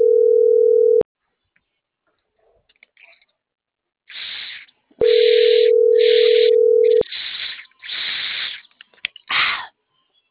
Ringback tone (North America), 440 + 480 Hz.